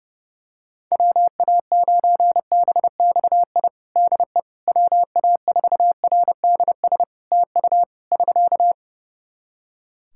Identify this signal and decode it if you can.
Morse code, heard as a keyed tone, about 31 wpm, 700 Hz. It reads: WA9BXI DE WA4RDS TU <SK>
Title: Practice QSO - Exchange 4 - Ending - 30wpm